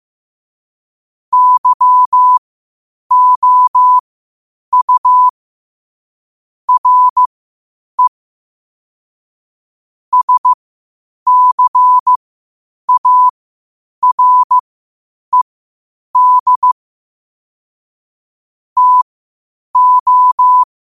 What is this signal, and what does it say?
Morse code, heard as a keyed tone, about 15 wpm, spaced as for 8 wpm, 1000 Hz. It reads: YOU RE SCARED TO
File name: qotdfiles / 8wpm